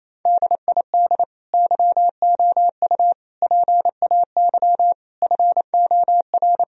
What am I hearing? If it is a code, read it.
Morse code, heard as a keyed tone, about 28 wpm, 700 Hz. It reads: DID YOU PAY FOR
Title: Sentences from Top 1000 Words - Review Part 3 - 28wpm